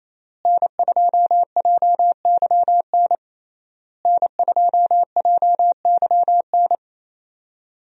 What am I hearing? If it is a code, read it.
Morse code, heard as a keyed tone, about 28 wpm, 700 Hz. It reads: N2JYN N2JYN
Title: USA Call Signs - 5 Characters - Repeated 3x - 28wpm